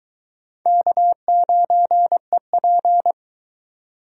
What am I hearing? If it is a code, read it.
Morse code, heard as a keyed tone, about 22 wpm, 700 Hz. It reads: K9EP